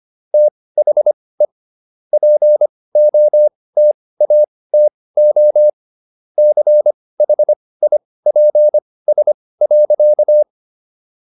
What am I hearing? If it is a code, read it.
Morse code, heard as a keyed tone, about 25 wpm, spaced as for 18 wpm, 600 Hz. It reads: THE POTATO CHIPS.